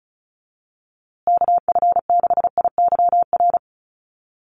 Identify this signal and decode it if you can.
Morse code, heard as a keyed tone, about 34 wpm, 700 Hz. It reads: KF6IYR
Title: USA Call Signs Repeated 3x 35wpm